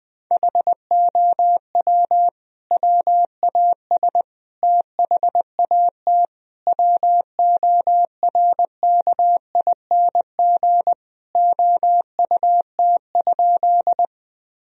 Morse code, heard as a keyed tone, about 20 wpm, 700 Hz. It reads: HOW WAS THAT WORKING OUT?
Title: Sentences from Top 1000 Words - Review Part 4 - 20wpm